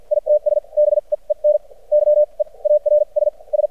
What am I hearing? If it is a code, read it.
Morse code, heard as a keyed tone, about 52 wpm, spaced as for 38 wpm, 600 Hz. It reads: ITS BEEN YEARS S
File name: archived / CRSnet